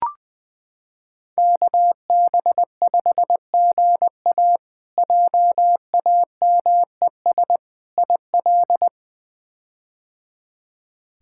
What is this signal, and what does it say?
Morse code, heard as a keyed tone, about 20 wpm, 700 Hz. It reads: KB5GA JAMES IL
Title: K1USN SST QSO Practice - 20wpm